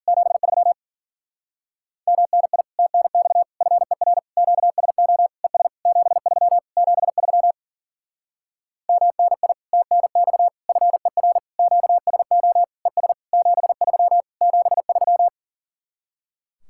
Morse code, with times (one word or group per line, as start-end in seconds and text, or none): 0.1-0.7 73
2.1-2.6 MNI
2.8-3.4 TNX
3.6-4.2 FER
4.4-5.3 QSO
5.4-5.7 ES
5.9-6.6 73
6.8-7.5 73
8.9-9.5 MNI
9.7-10.5 TNX
10.7-11.4 FER
11.6-12.6 QSO
12.9-13.1 ES
13.3-14.2 73
14.4-15.3 73